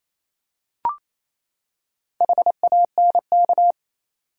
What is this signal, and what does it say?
Morse code, heard as a keyed tone, about 28 wpm, 700 Hz. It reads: HANK
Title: USA Common Names - Max 5 Letters - 28wpm